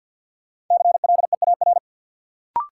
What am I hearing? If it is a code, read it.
Morse code, heard as a keyed tone, about 53 wpm, 700 Hz. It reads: CLEAR
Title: Sentences from Top 500 Words - New Word Drill - 50wpm